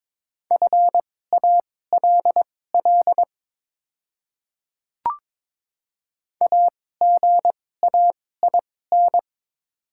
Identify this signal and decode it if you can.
Morse code, heard as a keyed tone, about 23 wpm, spaced as for 16 wpm, 700 Hz. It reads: FALL AGAIN
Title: 2x Character Spacing - Top 500 Words - 22wpm